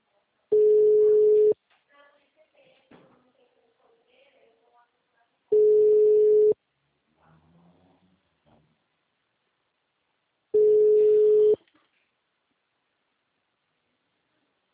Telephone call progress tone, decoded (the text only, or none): ringback tone